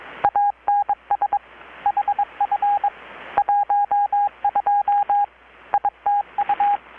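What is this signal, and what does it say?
Morse code, heard as a keyed tone, about 21 wpm, 820 Hz, with radio noise. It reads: ANS HF 12 ITU
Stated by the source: Morse from KPH's Night of Nights XX maritime radiotelegraphy event, received in Chiapas, Mexico about 3,800 km away on a Kenwood R-600 with a grounded 10 m long-wire antenna.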